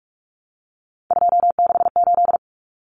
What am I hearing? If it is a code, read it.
Morse code, heard as a keyed tone, about 44 wpm, 700 Hz. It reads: 268
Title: Common Cut Numbers - 3-Digit 45wpm